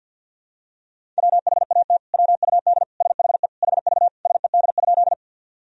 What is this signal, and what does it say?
Morse code, heard as a keyed tone, about 52 wpm, 700 Hz. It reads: WHAT WUD SHE HV SED?
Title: Sentences from Top 100 Words and Abbreviations 50wpm